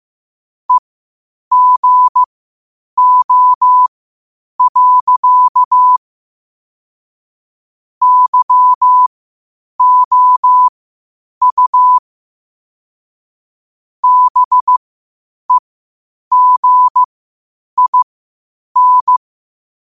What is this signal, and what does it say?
Morse code, heard as a keyed tone, about 15 wpm, spaced as for 8 wpm, 1000 Hz. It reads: EGO. YOU BEGIN
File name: qotdfiles / 8wpm